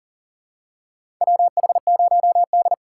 Morse code, heard as a keyed tone, about 37 wpm, 700 Hz. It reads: WH0D
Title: USA Call Signs - 6x Character Spacing - 40wpm